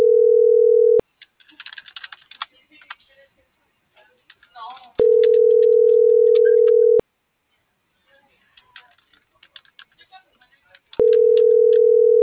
Ringback tone (North America), 440 + 480 Hz.